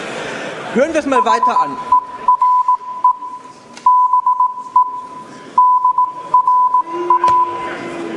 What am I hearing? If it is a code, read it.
Morse code, heard as a keyed tone, about 18 wpm, spaced as for 15 wpm, 1000 Hz, with speech. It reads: HERE BE DRA